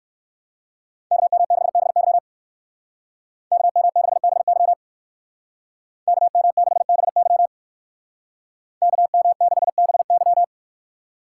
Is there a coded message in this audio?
Yes: Morse code from 1.1 to 10.4 s, heard as a keyed tone, about 58 wpm, 700 Hz. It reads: KM6BY KM6BY KM6BY KM6BY